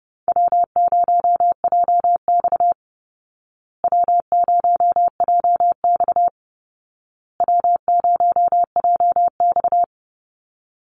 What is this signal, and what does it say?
Morse code, heard as a keyed tone, about 30 wpm, 700 Hz. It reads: W0JX W0JX W0JX
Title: Easy Call Signs Repeated 3x - 30wpm